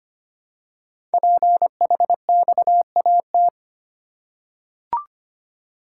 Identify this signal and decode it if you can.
Morse code, heard as a keyed tone, about 25 wpm, 700 Hz. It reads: PHXAT
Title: Code Groups - 5 Letter:Numbers - 25wpm